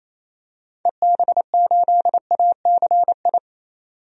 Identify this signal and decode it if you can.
Morse code, heard as a keyed tone, about 27 wpm, 700 Hz. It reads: EB8ACI
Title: DXCC-100 Call Signs - 6 Characters - 28wpm